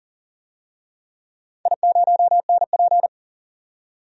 Morse code, heard as a keyed tone, about 40 wpm, 700 Hz. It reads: I0NP
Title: DXCC-100 Call Signs - 4 Characters - 40wpm